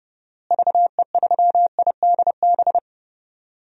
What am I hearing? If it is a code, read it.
Morse code, heard as a keyed tone, about 31 wpm, 700 Hz. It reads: VE3IDB
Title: DXCC-100 Call Signs - 6 Characters - Repeated 2x - 30wpm